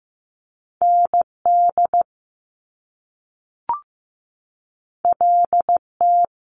Morse code, heard as a keyed tone, about 15 wpm, 700 Hz. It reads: ND LT
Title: Top Two Letter Combinations in Words 15wpm